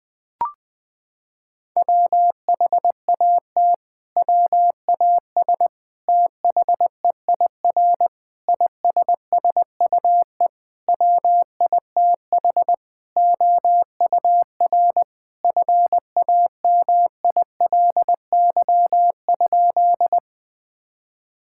Morse code, heard as a keyed tone, about 20 wpm, 700 Hz. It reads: WHAT WAS THEIR ISSUE WITH OUR FAMILY?
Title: Sentences from Top 1000 Words - Review Part 6 - 20wpm